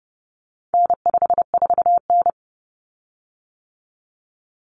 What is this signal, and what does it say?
Morse code, heard as a keyed tone, about 30 wpm, 700 Hz. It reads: N54N
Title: Common Cut Numbers - 4-Digit 30wpm